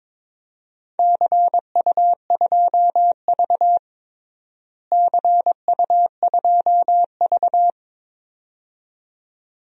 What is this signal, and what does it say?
Morse code, heard as a keyed tone, about 22 wpm, 700 Hz. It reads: CU2V CU2V
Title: DXCC-100 Call Signs - 4 Characters - Repeated 2x - 22wpm